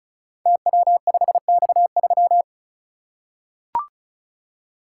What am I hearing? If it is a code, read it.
Morse code, heard as a keyed tone, about 36 wpm, 700 Hz. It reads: TW5X3
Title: Code Groups - 5 Letter:Numbers - 35wpm